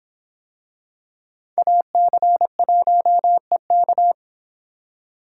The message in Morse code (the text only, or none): AC1EK